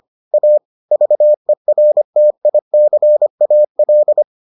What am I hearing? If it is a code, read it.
Morse code, heard as a keyed tone, about 25 wpm, 600 Hz. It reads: A VERTICAL